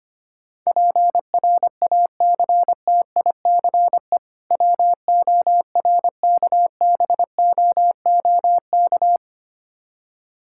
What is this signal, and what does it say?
Morse code, heard as a keyed tone, about 25 wpm, 700 Hz. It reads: PRACTICE WORKBOOK
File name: practice / 25wpm / Difficult Words - Speed-Racing with Hints - 25wpm